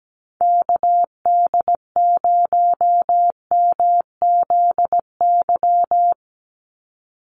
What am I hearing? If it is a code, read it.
Morse code, heard as a keyed tone, about 17 wpm, 700 Hz. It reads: KD0MZY
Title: USA Call Signs 17wpm